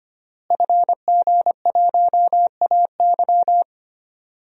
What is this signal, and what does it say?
Morse code, heard as a keyed tone, about 25 wpm, 700 Hz. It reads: FG1AY
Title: DXCC-100 Call Signs - 5 Characters - Repeated 3x - 25wpm